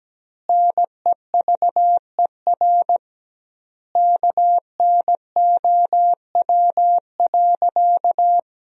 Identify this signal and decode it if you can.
Morse code, heard as a keyed tone, about 17 wpm, 700 Hz. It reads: NEVER KNOW.
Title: 2x Word-Spacing - Sentences from Top 300 Words - 17wpm